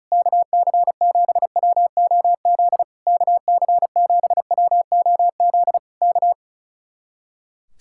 Morse code, heard as a keyed tone, about 33 wpm, 700 Hz. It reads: KC7WOZ KC7WOZ K
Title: Speed-Racing Common QSO Phrases - 35wpm